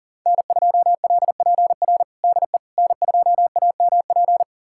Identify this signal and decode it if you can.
Morse code, heard as a keyed tone, about 40 wpm, 700 Hz. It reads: N2LPR DE N2AMP
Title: USA Callsign DE Callsign - 5 Characters - 40wpm